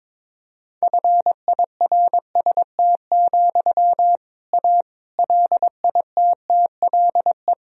Morse code, heard as a keyed tone, about 22 wpm, 700 Hz. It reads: FIRST, A LITTLE
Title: Sentences from Top 1000 Words - Review Part 2 - 22wpm